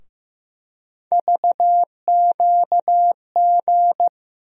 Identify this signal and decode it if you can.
Morse code, heard as a keyed tone, about 15 wpm, 700 Hz. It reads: VQG
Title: USA Call Sign Postfixes - 3 Letters 15wpm